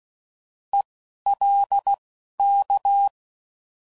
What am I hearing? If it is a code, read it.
Morse code, heard as a keyed tone, about 16 wpm, spaced as for 11 wpm, 800 Hz. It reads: ELK